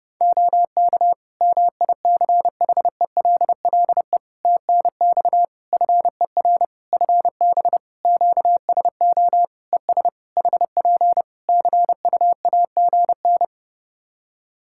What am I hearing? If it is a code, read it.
Morse code, heard as a keyed tone, about 32 wpm, 700 Hz. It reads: OK MICHELLE TNX FER FB QSO ES HP CUAGN
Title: Practice QSO - Exchange 4 - Ending - 30wpm